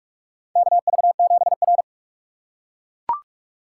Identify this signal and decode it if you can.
Morse code, heard as a keyed tone, about 44 wpm, 700 Hz. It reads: KV7R
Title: USA Call Signs - 4 Characters - 3x Character Spacing - 45wpm